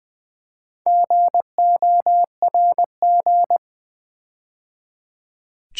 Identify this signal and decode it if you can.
Morse code, heard as a keyed tone, about 20 wpm, 700 Hz. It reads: GORG
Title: Build A Word - 4 Letters At A Time - 8-Letter Words - 20wpm